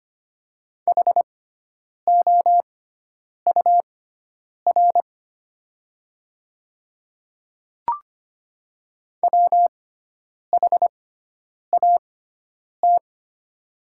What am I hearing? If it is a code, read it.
Morse code, heard as a keyed tone, about 25 wpm, spaced as for 9 wpm, 700 Hz. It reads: HOUR WHAT
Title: 6x Character Spacing - Top 500 Words - 25wpm